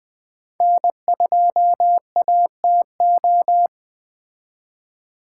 Morse code, heard as a keyed tone, about 20 wpm, 700 Hz. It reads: N2ATO
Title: Speed Racing - USA 1x3 Call Signs 17wpm